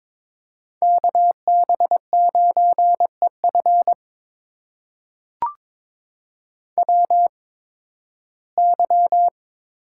Morse code, heard as a keyed tone, about 22 wpm, spaced as for 14 wpm, 700 Hz. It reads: KB9EF WY
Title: USA Call Signs - 8x Character Spacing - 22wpm